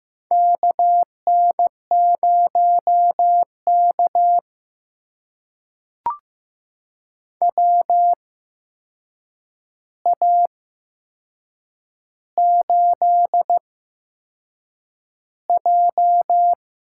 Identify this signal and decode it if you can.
Morse code, heard as a keyed tone, about 15 wpm, spaced as for 6 wpm, 700 Hz. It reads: KN0K WA8J
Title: USA Call Signs - 4 Characters - 8x Character Spacing - 15wpm